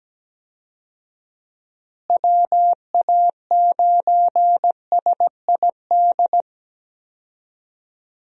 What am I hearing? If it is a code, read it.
Morse code, heard as a keyed tone, about 17 wpm, 700 Hz. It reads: WA9SID
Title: USA Call Signs - 6 Characters - Repeated 2x - 17wpm